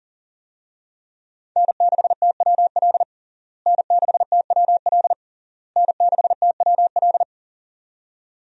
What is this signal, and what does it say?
Morse code, heard as a keyed tone, about 40 wpm, 700 Hz. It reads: N6TWL N6TWL N6TWL
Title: USA Call Signs - 5 Characters - Repeated 3x - 40wpm